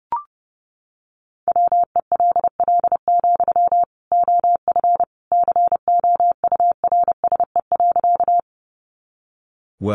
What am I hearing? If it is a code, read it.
Morse code, heard as a keyed tone, about 28 wpm, 700 Hz. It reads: WELL, OF COURSE.
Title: Sentences from Top 500 Words 30wpm